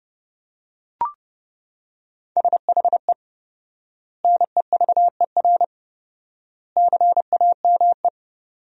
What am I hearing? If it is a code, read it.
Morse code, heard as a keyed tone, about 32 wpm, 700 Hz. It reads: SHE NEVER CAME
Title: 4x Word-Spacing - Sentences from Top 300 Words - 30wpm